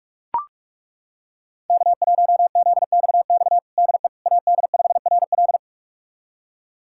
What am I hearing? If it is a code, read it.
Morse code, heard as a keyed tone, about 41 wpm, 700 Hz. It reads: K1ZXX DE AD5RL
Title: USA Callsign DE Callsign - 5 Characters - 45wpm